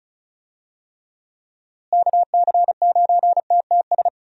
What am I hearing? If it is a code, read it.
Morse code, heard as a keyed tone, about 34 wpm, 700 Hz. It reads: KC9TTS
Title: USA Call Signs - 6 Characters - Repeated 3x - 35wpm